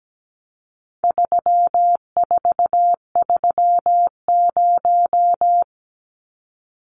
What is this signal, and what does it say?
Morse code, heard as a keyed tone, about 17 wpm, 700 Hz. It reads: 3430
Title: Numbers - 4-Digit 17wpm